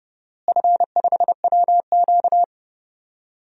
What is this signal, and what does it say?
Morse code, heard as a keyed tone, about 30 wpm, 700 Hz. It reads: F5WQ
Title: DXCC-100 Call Signs - 4 Characters - Repeated 2x - 30wpm